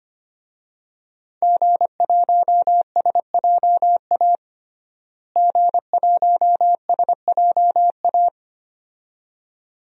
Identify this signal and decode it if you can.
Morse code, heard as a keyed tone, about 25 wpm, 700 Hz. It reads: G1SJA G1SJA
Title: DXCC-100 Call Signs - 5 Characters - Repeated 2x - 25wpm